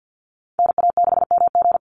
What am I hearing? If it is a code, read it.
Morse code, heard as a keyed tone, about 50 wpm, 700 Hz. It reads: NA6MG
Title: NA SOTA Chaser Callsigns 50wpm